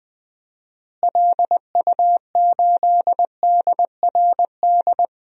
Morse code, heard as a keyed tone, about 20 wpm, 700 Hz. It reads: LU8DRD
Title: DXCC-100 Call Signs - 6 Characters - Repeated 3x - 20wpm